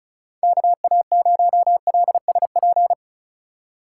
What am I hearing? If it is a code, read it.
Morse code, heard as a keyed tone, about 36 wpm, 700 Hz. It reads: KA0LSP
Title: USA Call Signs - 6 Characters - 8x Character Spacing - 35wpm